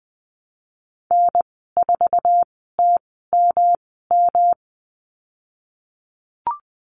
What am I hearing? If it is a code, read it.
Morse code, heard as a keyed tone, about 20 wpm, spaced as for 15 wpm, 700 Hz. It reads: N4TMM
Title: 173 Lesson 28g - Number 9 - Call Signs 20wpm